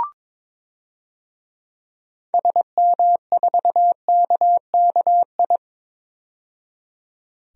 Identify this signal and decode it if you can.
Morse code, heard as a keyed tone, about 22 wpm, 700 Hz. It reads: SM4KKI
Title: DXCC-100 Call Signs - 6 Characters - 22wpm